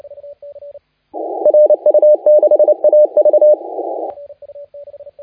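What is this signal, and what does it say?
Morse code, heard as a keyed tone, about 27 wpm, 590 Hz. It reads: RU6AV
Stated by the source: Morse contest audio recorded by LA4C during the SAC CW 2018 contest.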